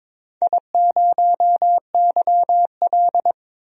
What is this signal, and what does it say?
Morse code, heard as a keyed tone, about 22 wpm, 700 Hz. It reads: I0YL
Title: DXCC-100 Call Signs - 4 Characters - 22wpm